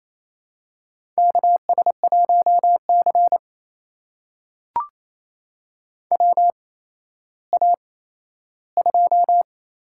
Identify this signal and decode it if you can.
Morse code, heard as a keyed tone, about 28 wpm, spaced as for 14 wpm, 700 Hz. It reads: KS1C WA2
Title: USA Call Signs - 8x Character Spacing - 28wpm